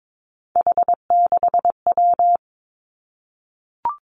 Morse code, heard as a keyed tone, about 22 wpm, 700 Hz. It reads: H6W
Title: Three Challenging Letter-Number Combinations 22wpm